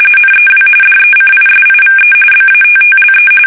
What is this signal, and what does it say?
RTTY, about 45.45 baud. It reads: NEW DECISIVE BATTLE
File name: fsk850 / battleshipnj